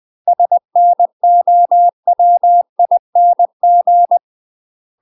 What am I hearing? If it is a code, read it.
Morse code, heard as a keyed tone, about 20 wpm, 700 Hz. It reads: SNOWING
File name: UK MASTER QSOs TRAINING / 20WPM / UK MASTER QSOs 0065